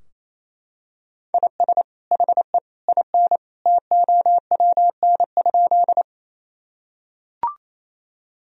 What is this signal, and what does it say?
Morse code, heard as a keyed tone, about 28 wpm, 700 Hz. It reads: IS HE IN TOWN?
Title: Sentences from Top 1000 Words - Review Part 2 - 28wpm